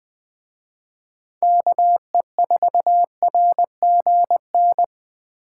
Morse code, heard as a keyed tone, about 20 wpm, 700 Hz. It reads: KE4RGN
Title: Speed Racing - USA 2x3 Call Signs 17wpm